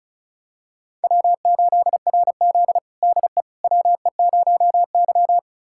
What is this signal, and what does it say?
Morse code, heard as a keyed tone, about 36 wpm, 700 Hz. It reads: W8RZ DE WE0Y
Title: USA Callsign DE Callsign - 4 Characters - 35wpm